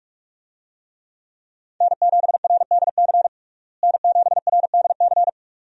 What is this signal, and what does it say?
Morse code, heard as a keyed tone, about 46 wpm, 700 Hz. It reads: N7RDC N7RDC
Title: USA Call Signs - 5 Characters - Repeated 2x - 45wpm